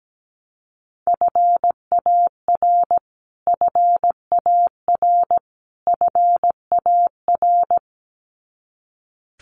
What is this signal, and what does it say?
Morse code, heard as a keyed tone, about 17 wpm, 700 Hz. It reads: FAR FAR FAR
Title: Top 300 Words Repeated 17wpm